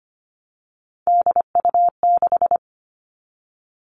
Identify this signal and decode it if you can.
Morse code, heard as a keyed tone, about 25 wpm, 700 Hz. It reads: DU6
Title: Three Challenging Letter-Number Combinations 25wpm